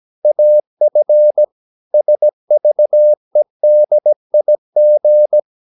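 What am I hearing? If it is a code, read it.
Morse code, heard as a keyed tone, about 17 wpm, 600 Hz. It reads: AF SVEDIG
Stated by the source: Slow (QRS) Morse practice at 17 wpm, from CW med Gnister.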